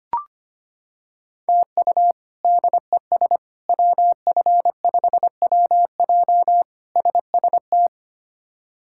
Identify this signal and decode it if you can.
Morse code, heard as a keyed tone, about 25 wpm, 700 Hz. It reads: TU DES WF5WJ SST
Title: K1USN SST QSO Practice - 25wpm